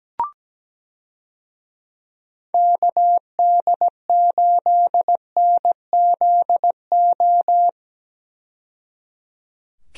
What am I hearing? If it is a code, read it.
Morse code, heard as a keyed tone, about 17 wpm, 700 Hz. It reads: KD8NZO
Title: USA Call Signs - 6 Characters - 17wpm